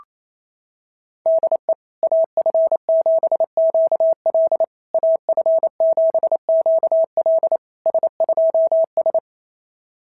Morse code, heard as a keyed tone, about 27 wpm, 690 Hz. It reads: DE AF7QL AF7QL S2S
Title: SOTA QSO Practice 28wpm